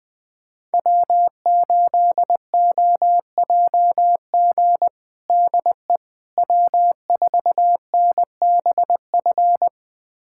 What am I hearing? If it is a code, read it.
Morse code, heard as a keyed tone, about 20 wpm, 700 Hz. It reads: W8OJG DE W4NBF
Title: USA Callsign DE Callsign - 5 Characters - 20wpm